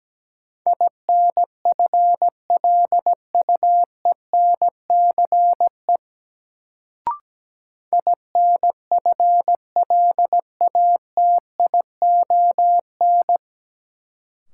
Morse code, heard as a keyed tone, about 17 wpm, 700 Hz. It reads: INFLUENCE INFLATION
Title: Sentences from Top 1100 Words - New Word Drill Alt - 17wpm